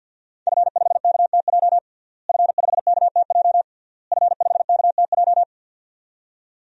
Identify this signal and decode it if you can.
Morse code, heard as a keyed tone, about 52 wpm, 700 Hz. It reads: F5KTJ F5KTJ F5KTJ
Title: DXCC-100 Call Signs - 5 Characters - Repeated 3x - 50wpm